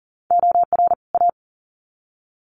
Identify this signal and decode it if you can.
Morse code, heard as a keyed tone, about 40 wpm, 700 Hz. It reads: OR A
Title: Sets of 2 Words 40wpm